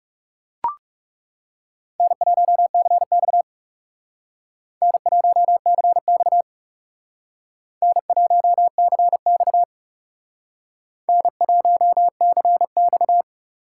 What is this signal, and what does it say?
Morse code, heard as a keyed tone, about 40 wpm, 700 Hz. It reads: N1CX N1CX N1CX N1CX